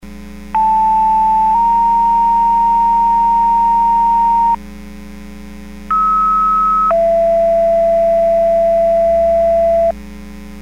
Two-tone page, heard as a one-second tone, then a three-second tone, about 880 then 930 Hz, about 1290 then 670 Hz. It is heard from 0.5 to 9.9 s.